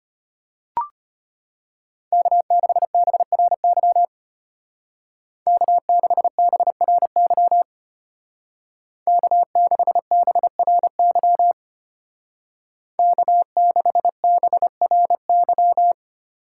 Morse code, heard as a keyed tone, about 31 wpm, 700 Hz. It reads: K6BRY K6BRY K6BRY K6BRY
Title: Speed Racing - USA 1x3 Call Signs 25wpm